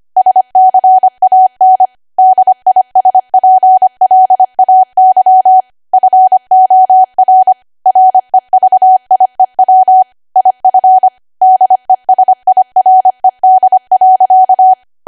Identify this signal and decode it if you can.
Morse code, heard as a keyed tone, about 25 wpm, 750 Hz. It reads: SCAN DISPLAY FOR REVIEW IF DESIRED.
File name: Archive / 25 WPM / 25WPM